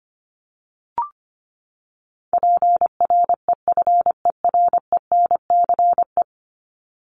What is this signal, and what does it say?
Morse code, heard as a keyed tone, about 25 wpm, 700 Hz. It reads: PREFERENCE
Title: Top 1251-1500 Words 25wpm